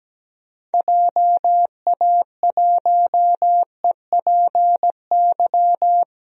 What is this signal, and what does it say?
Morse code, heard as a keyed tone, about 17 wpm, 700 Hz. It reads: JA1EPY
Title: DXCC-100 Call Signs - 6 Characters - Repeated 2x - 17wpm